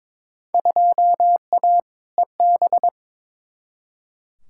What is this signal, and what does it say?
Morse code, heard as a keyed tone, about 22 wpm, 700 Hz. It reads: 2A EB
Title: Field Day Exchanges - 22wpm